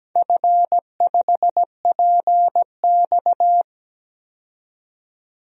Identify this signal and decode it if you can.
Morse code, heard as a keyed tone, about 17 wpm, 700 Hz. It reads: F5PX
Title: DXCC-100 Call Signs - 4 Characters - 17wpm